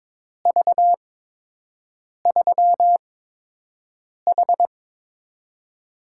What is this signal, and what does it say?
Morse code, heard as a keyed tone, about 22 wpm, spaced as for 6 wpm, 700 Hz. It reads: V3H
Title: USA Call Signs - 4 Characters - 8x Character Spacing - 22wpm